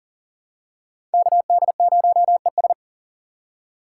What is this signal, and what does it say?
Morse code, heard as a keyed tone, about 40 wpm, 700 Hz. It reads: KD0ES
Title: USA Call Signs - 5 Characters - 4x Character Spacing - 40wpm